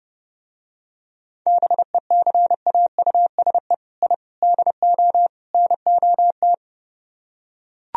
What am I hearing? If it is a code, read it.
Morse code, heard as a keyed tone, about 30 wpm, 700 Hz. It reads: BECAUSE I DO NOT